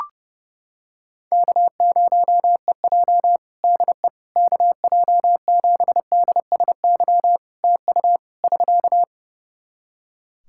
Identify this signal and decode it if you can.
Morse code, heard as a keyed tone, about 29 wpm, 700 Hz. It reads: K0EJ DE KJ7DSY TU <SK>
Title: Practice QSO - Exchange 4 - Ending - 30wpm